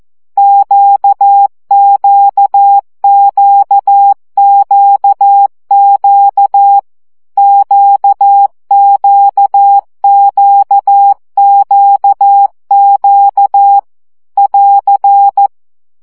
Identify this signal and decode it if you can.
Morse code, heard as a keyed tone, about 14 wpm, 800 Hz. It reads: QQQQQ QQQQQ <AR>
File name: CW gyakorlas q 12 baud